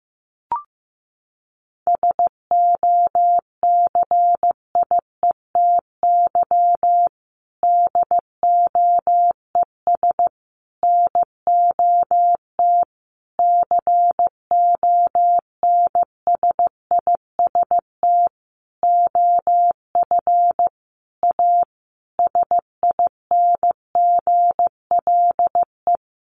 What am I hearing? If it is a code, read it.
Morse code, heard as a keyed tone, about 15 wpm, 700 Hz. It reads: SOCIETY DOES NOT CONSIST OF A SINGLE
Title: Sentences from Top 1400 Words 15wpm